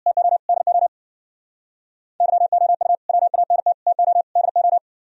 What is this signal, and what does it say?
Morse code, heard as a keyed tone, about 60 wpm, 700 Hz. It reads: TO DO YOU WANT TO DO
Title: Speed-Racing - Sets of 4 Words 50wpm